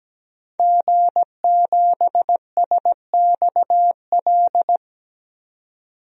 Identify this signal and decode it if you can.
Morse code, heard as a keyed tone, about 17 wpm, 700 Hz. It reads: G7SXL
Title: DXCC-100 Call Signs - 5 Characters - 17wpm